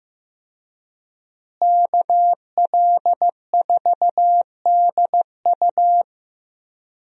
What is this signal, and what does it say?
Morse code, heard as a keyed tone, about 15 wpm, 700 Hz. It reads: KL4DU
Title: USA Call Signs - 5 Characters - Repeated 3x - 15wpm